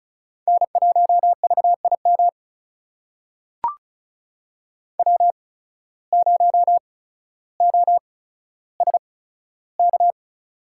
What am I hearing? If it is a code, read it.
Morse code, heard as a keyed tone, about 36 wpm, spaced as for 15 wpm, 700 Hz. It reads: N1VIM W0OSK
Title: USA Call Signs - 5 Characters - 8x Character Spacing - 35wpm